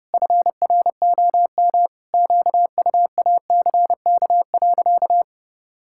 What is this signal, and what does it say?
Morse code, heard as a keyed tone, about 30 wpm, 700 Hz. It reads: FROM QUACK.
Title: Pangram Challenge v2 - 30wpm